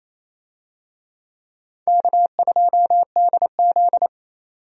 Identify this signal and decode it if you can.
Morse code, heard as a keyed tone, about 28 wpm, 700 Hz. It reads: K2DZ
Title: USA Call Signs - 4 Characters - 28wpm